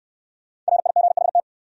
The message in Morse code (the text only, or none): RIGHT